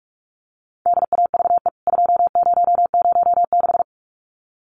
Morse code, heard as a keyed tone, about 44 wpm, 700 Hz. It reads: DAVE 2006
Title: CWT Exchange 45wpm